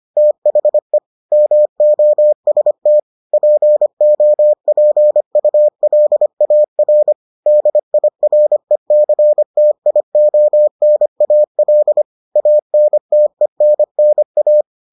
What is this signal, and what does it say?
Morse code, heard as a keyed tone, about 25 wpm, 600 Hz. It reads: THE MOST POPULAR DIRECTIONAL ANTENNA